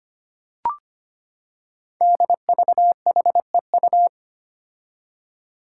Morse code, heard as a keyed tone, about 25 wpm, 700 Hz. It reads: DVHEU
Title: Five Challenging Letter-Number Combinations 25wpm